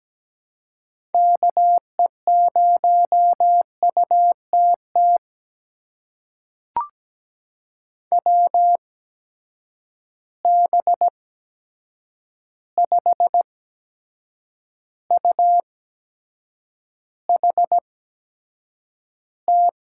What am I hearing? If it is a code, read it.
Morse code, heard as a keyed tone, about 17 wpm, spaced as for 7 wpm, 700 Hz. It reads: KE0UTT WB5UHT
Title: USA Call Signs - 6 Characters - 8x Character Spacing - 17wpm